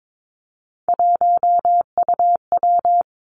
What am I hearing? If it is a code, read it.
Morse code, heard as a keyed tone, about 22 wpm, 700 Hz. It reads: 1UW
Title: Three Challenging Letter-Number Combinations 22wpm